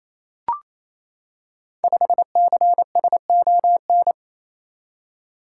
Morse code, heard as a keyed tone, about 29 wpm, 700 Hz. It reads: 5CSON
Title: Code Groups - 5 Letter:Numbers - 28wpm